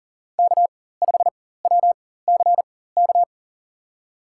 Morse code, heard as a keyed tone, about 40 wpm, spaced as for 19 wpm, 700 Hz. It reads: K5WCK
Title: USA Call Signs - 5 Characters - 4x Character Spacing - 40wpm